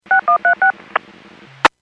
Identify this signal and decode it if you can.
DTMF, heard as telephone touch tones, tones of about 110 ms. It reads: 6136
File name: Businesses / 03184 01-51 Businesses 451.25 WPS